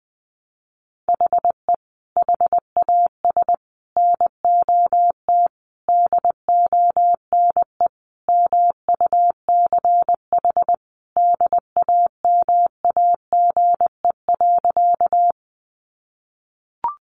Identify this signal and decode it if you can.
Morse code, heard as a keyed tone, about 20 wpm, 700 Hz. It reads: HE HAS NOT DONE MUCH DAMAGE.
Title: Sentences from Top 1100 Words 20wpm